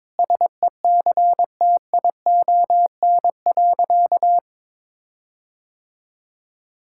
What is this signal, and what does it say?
Morse code, heard as a keyed tone, about 22 wpm, 700 Hz. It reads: SECTION.
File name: Sentences with 9-Word Phrases - 22wpm